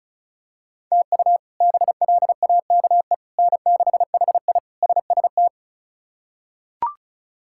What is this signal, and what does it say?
Morse code, heard as a keyed tone, about 34 wpm, 700 Hz. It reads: TU BLAKE N6HI SST
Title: K1USN SST QSO Practice - 35wpm